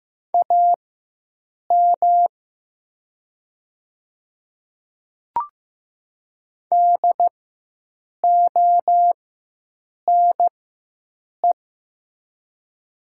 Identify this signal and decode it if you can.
Morse code, heard as a keyed tone, about 15 wpm, spaced as for 7 wpm, 700 Hz. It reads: AM DONE
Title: 4x Character Spacing - Top 500 Words - 15wpm